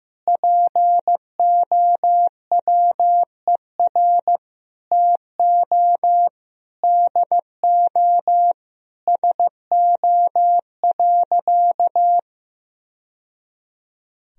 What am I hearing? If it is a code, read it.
Morse code, heard as a keyed tone, about 15 wpm, 700 Hz. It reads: POWER TO DO SO.
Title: Sentences from Top 1000 Words - Review Part 4 - 15wpm